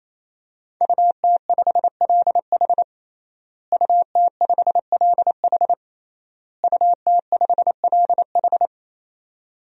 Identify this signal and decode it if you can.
Morse code, heard as a keyed tone, about 28 wpm, 700 Hz. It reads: UT5LH UT5LH UT5LH
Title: DXCC-100 Call Signs - 5 Characters - Repeated 3x - 28wpm